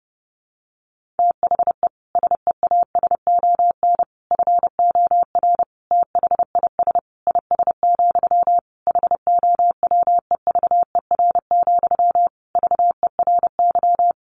Morse code, heard as a keyed tone, about 30 wpm, 700 Hz. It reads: THE SEASON FOR THIS IS, HOWEVER, VERY
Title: Sentences from Top 1400 Words 30wpm